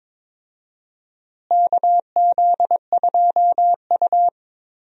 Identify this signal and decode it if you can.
Morse code, heard as a keyed tone, about 22 wpm, 700 Hz. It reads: KZ2U
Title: USA Call Signs - 4 Characters - Repeated 2x - 22wpm